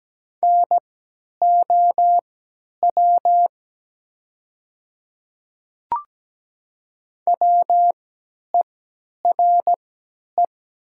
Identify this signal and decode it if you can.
Morse code, heard as a keyed tone, about 17 wpm, spaced as for 10 wpm, 700 Hz. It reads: NOW WERE
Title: 3x Character Spacing - Top 100 Words - 17wpm